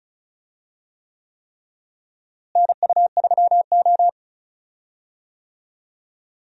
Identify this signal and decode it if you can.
Morse code, heard as a keyed tone, about 34 wpm, 700 Hz. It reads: NU3O
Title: USA Call Signs - 4 Characters - Repeated 3x - 35wpm